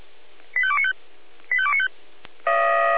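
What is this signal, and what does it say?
Five-tone selective call (ZVEI), heard as quick runs of five tones, tones of about 70 ms. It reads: 85295 85295
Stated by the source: Received on an SDR; ZVEI five-tone selective call, code 85242.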